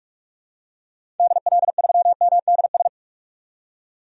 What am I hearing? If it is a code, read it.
Morse code, heard as a keyed tone, about 46 wpm, 700 Hz. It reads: DL3MDS
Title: DXCC-100 Call Signs - 6 Characters - 45wpm